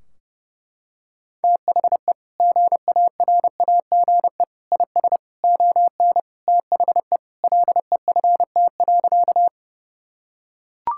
Morse code, heard as a keyed tone, about 30 wpm, 700 Hz. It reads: THE GARAGE IS ON THE LEFT.
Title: Sentences from Top 1600 Words 30wpm